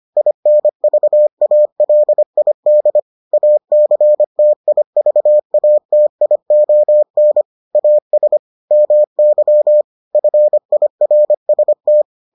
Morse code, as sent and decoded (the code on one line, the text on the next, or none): .. -. ...- .- .-.. .. -.. / .- -.-. - .. ...- .- - .. --- -. / .- ... / -- -.-- / ..-. .. .-. ... -
INVALID ACTIVATION AS MY FIRST